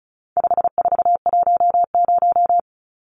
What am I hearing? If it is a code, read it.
Morse code, heard as a keyed tone, about 34 wpm, 700 Hz. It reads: 5410
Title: Numbers - 4-Digit 35wpm